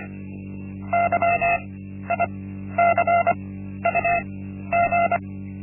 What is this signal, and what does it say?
Morse code, heard as a keyed tone, about 25 wpm, spaced as for 12 wpm, 690 Hz. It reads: YICUG
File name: Linksmakalnis-1